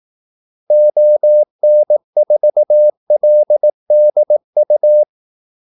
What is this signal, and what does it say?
Morse code, heard as a keyed tone, about 18 wpm, 600 Hz. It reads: ON4LDU
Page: Kaldesignaler 18 wpm